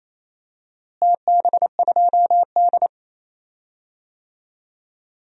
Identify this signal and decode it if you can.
Morse code, heard as a keyed tone, about 28 wpm, 700 Hz. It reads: TB2D
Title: DXCC-100 Call Signs - 4 Characters - Repeated 3x - 28wpm